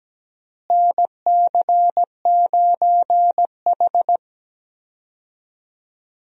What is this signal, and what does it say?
Morse code, heard as a keyed tone, about 17 wpm, 700 Hz. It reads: NC9H
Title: USA Call Signs - 4 Characters - Repeated 3x - 17wpm